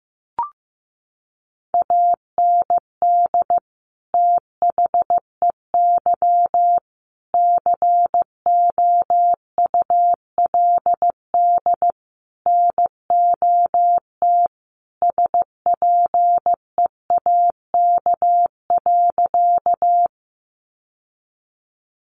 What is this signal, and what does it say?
Morse code, heard as a keyed tone, about 15 wpm, 700 Hz. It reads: AND THEY COULD NOT SPEAK.
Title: Sentences from Top 1100 Words 15wpm